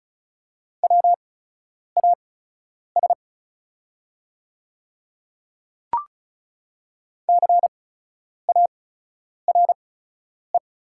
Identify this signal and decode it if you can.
Morse code, heard as a keyed tone, about 36 wpm, spaced as for 10 wpm, 700 Hz. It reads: WAS CARE
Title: 8x Character Spacing - Top 300 Words - 35wpm